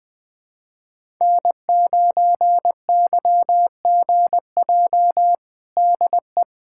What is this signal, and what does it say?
Morse code, heard as a keyed tone, about 20 wpm, 700 Hz. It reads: N9YGJ DE
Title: USA Callsign DE Callsign 20wpm